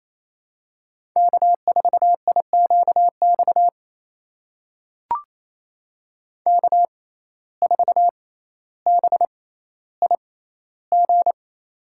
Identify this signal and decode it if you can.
Morse code, heard as a keyed tone, about 29 wpm, spaced as for 15 wpm, 700 Hz. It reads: K4IQX K4BIG
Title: USA Call Signs - 5 Characters - 6x Character Spacing - 28wpm